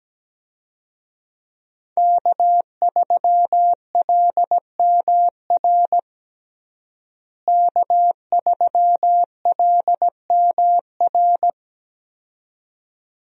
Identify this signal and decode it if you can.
Morse code, heard as a keyed tone, about 17 wpm, 700 Hz. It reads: K3LMR K3LMR
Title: USA Call Signs - 5 Characters - Repeated 2x - 17wpm